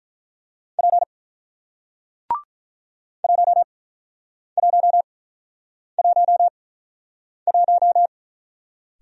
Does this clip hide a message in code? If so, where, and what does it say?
Yes: Morse code from 0.8 to 8.1 s, heard as a keyed tone, about 48 wpm, spaced as for 9 wpm, 700 Hz. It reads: P 1111